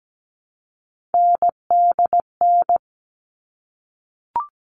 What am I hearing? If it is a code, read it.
Morse code, heard as a keyed tone, about 17 wpm, 700 Hz. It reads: NDN
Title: All Cut Numbers - 3-Digit 17wpm